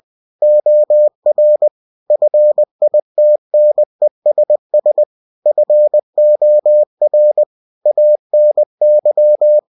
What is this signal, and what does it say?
Morse code, heard as a keyed tone, about 20 wpm, 600 Hz. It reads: OR FITNESS FOR ANY
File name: cw-books / winnie-the-pooh / winnie-20wpm-page-0119